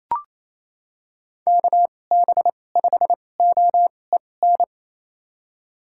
Morse code, heard as a keyed tone, about 28 wpm, spaced as for 20 wpm, 700 Hz. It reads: KB5OEN
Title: USA Call Signs - 6 Characters - 2x Character Spacing - 28wpm